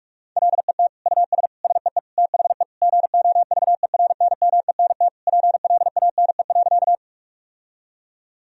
Morse code, heard as a keyed tone, about 46 wpm, 700 Hz. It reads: LET US SEE THE GOVERNMENT PLANE.
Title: Sentences from Top 1000 Words - Review Part 3 - 45wpm